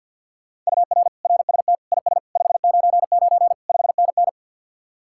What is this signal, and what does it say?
Morse code, heard as a keyed tone, about 65 wpm, spaced as for 57 wpm, 700 Hz. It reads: UR RST IS 599 5NN
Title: Speed-Racing Common QSO Phrases - 50wpm